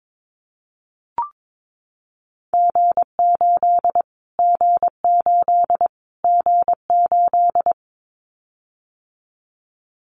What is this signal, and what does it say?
Morse code, heard as a keyed tone, about 22 wpm, 700 Hz. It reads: G8 G8 G8
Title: Callsign Prefixes for Top 25 DXCC - Repeated 3x - 22wpm